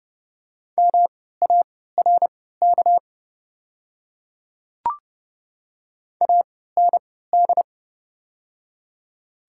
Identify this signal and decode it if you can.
Morse code, heard as a keyed tone, about 30 wpm, spaced as for 17 wpm, 700 Hz. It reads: MARK AND
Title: 3x Character Spacing - Top 500 Words - 30wpm